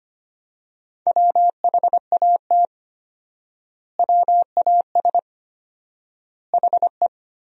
Morse code, heard as a keyed tone, about 25 wpm, 700 Hz. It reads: WHAT WAS HE
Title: 4x Word-Spacing - Sentences from Top 100 Words - 25wpm